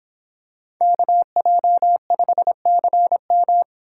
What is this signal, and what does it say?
Morse code, heard as a keyed tone, about 25 wpm, 700 Hz. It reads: KJ5CM